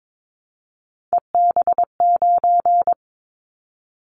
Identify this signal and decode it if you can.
Morse code, heard as a keyed tone, about 22 wpm, 700 Hz. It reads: EB9
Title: Callsign Prefixes for Top 100 DXCC - 22wpm